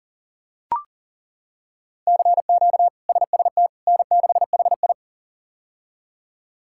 Morse code, heard as a keyed tone, about 37 wpm, 700 Hz. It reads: CQ SST N6HI
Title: K1USN SST QSO Practice - 40wpm